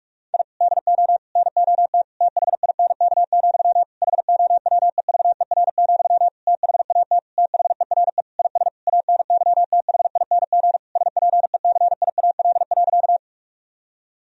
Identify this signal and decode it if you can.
Morse code, heard as a keyed tone, about 43 wpm, 700 Hz. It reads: I DO NOT THINK, HOWEVER, THAT THERE IS ANYTHING SPECIAL.
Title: Sentences from Top 1000 Words - Review Part 3 - 45wpm